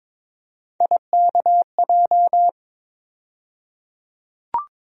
Morse code, heard as a keyed tone, about 22 wpm, 700 Hz. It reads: IKJ
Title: USA Call Sign Postfixes - 3 Letters 22wpm